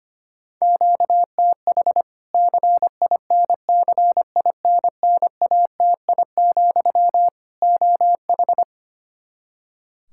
Morse code, heard as a keyed tone, about 25 wpm, 700 Hz. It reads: QTH CINCINNATI, OH
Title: Practice QSO - Exchange 1 - Beginning - 25wpm